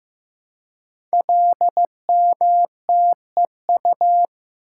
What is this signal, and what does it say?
Morse code, heard as a keyed tone, about 15 wpm, 700 Hz. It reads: LMTEU
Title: Code Groups - 5 Letters - 15wpm